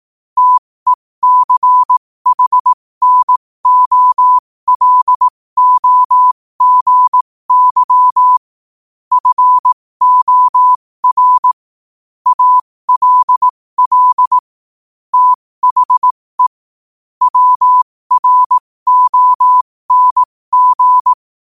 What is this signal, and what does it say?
Morse code, heard as a keyed tone, about 18 wpm, spaced as for 15 wpm, 1000 Hz. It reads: TECHNOLOGY FOR ALL THE WRONG
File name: qotdfiles / 15wpm